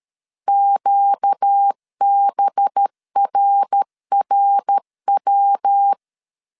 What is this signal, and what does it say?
Morse code, heard as a keyed tone, about 13 wpm, 800 Hz. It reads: QBRRW
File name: CW / DET / 1 6 1500C S B 12wpm 3431